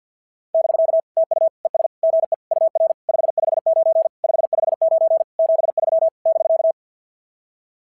Morse code, heard as a keyed tone, about 53 wpm, 650 Hz. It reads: <BK> TU ES GE UR 559 559 73 <BK>
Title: SOTA QSO Practice 50wpm